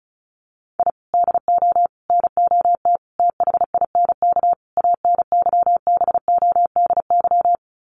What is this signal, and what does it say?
Morse code, heard as a keyed tone, about 34 wpm, 700 Hz. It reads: I DO NOT THINK ANYBODY
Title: Sets of 5 Words 35wpm